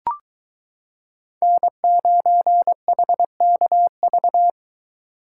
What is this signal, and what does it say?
Morse code, heard as a keyed tone, about 24 wpm, 700 Hz. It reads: N9HKV